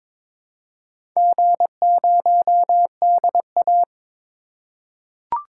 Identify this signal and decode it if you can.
Morse code, heard as a keyed tone, about 22 wpm, 700 Hz. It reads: G0DA
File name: practice / 22wpm / Four Letter-Number Combinations - 22wpm